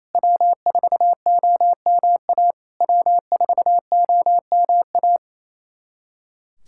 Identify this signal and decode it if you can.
Morse code, heard as a keyed tone, about 28 wpm, 690 Hz. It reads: W4OMA W4OMA
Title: SOTA QSO Practice 28wpm